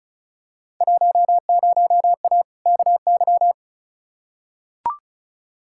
Morse code, heard as a keyed tone, about 36 wpm, 700 Hz. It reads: 10A KY
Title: Field Day Exchanges Encore - 35wpm